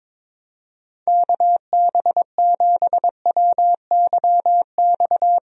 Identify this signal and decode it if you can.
Morse code, heard as a keyed tone, about 22 wpm, 700 Hz. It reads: KB7WYX
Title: USA Call Signs - 6 Characters - Repeated 2x - 22wpm